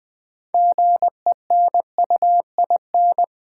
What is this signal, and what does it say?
Morse code, heard as a keyed tone, about 20 wpm, 700 Hz. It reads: GENUIN
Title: Build A Word - 3 Letters At A Time - 9-Letter Words - 20wpm